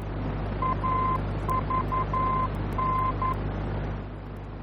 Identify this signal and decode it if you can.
Morse code, heard as a keyed tone, about 11 wpm, 1040 Hz, with noise. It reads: AVN
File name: france / mx / AVN-F-VOR